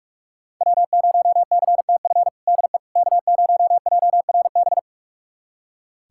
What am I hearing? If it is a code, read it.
Morse code, heard as a keyed tone, about 41 wpm, 700 Hz. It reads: W0CTF DE K0JRB
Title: USA Callsign DE Callsign - 5 Characters - 45wpm